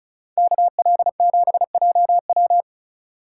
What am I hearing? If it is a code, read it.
Morse code, heard as a keyed tone, about 36 wpm, 700 Hz. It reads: KL7JW
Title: Speed Racing - USA 2x2 Call Signs 30wpm